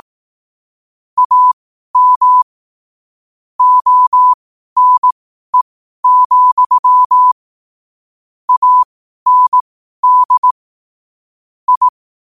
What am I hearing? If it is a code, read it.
Morse code, heard as a keyed tone, about 18 wpm, spaced as for 12 wpm, 1000 Hz. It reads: AM ONE, AND I
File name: qotdfiles / 12wpm